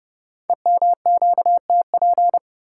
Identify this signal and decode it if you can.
Morse code, heard as a keyed tone, about 30 wpm, 700 Hz. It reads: EMQTP
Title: Code Groups - 5 Letters - 30wpm